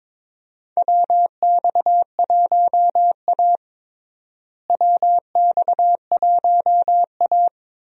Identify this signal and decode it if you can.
Morse code, heard as a keyed tone, about 21 wpm, 700 Hz. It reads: WX1A WX1A
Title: USA Call Signs - 4 Characters - Repeated 2x - 22wpm